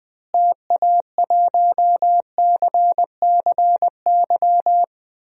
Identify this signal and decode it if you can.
Morse code, heard as a keyed tone, about 20 wpm, 700 Hz. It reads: TA1CCY
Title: DXCC-100 Call Signs - 6 Characters - Repeated 2x - 20wpm